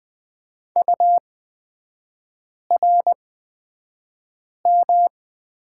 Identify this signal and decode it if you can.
Morse code, heard as a keyed tone, about 20 wpm, spaced as for 5 wpm, 700 Hz. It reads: URM
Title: 229 Lesson 37e - Letter x - Three Characters 20wpm